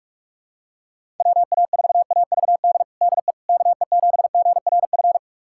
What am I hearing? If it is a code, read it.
Morse code, heard as a keyed tone, about 44 wpm, 700 Hz. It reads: WA4AVD DE KE7GRF
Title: USA Callsign DE Callsign 45wpm